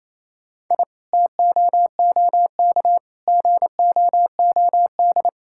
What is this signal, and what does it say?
Morse code, heard as a keyed tone, about 28 wpm, 700 Hz. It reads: I TOOK GOOD
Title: Sentences from Top 1000 Words - Review Part 6 - 28wpm